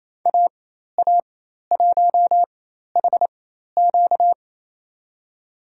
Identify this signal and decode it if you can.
Morse code, heard as a keyed tone, about 28 wpm, spaced as for 13 wpm, 700 Hz. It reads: AA1HQ
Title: USA Call Signs - 5 Characters - 4x Character Spacing - 28wpm